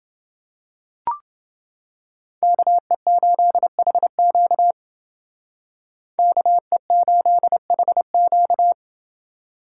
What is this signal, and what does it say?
Morse code, heard as a keyed tone, about 28 wpm, 700 Hz. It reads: KE8HQ KE8HQ